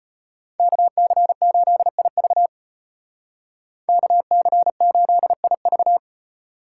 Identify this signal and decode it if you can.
Morse code, heard as a keyed tone, about 32 wpm, 700 Hz. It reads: KC8IV KC8IV